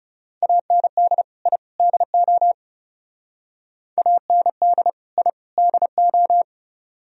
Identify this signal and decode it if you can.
Morse code, heard as a keyed tone, about 34 wpm, 700 Hz. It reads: AND I DO AND I DO
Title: Speed-Racing - Sets of 3 Words 30wpm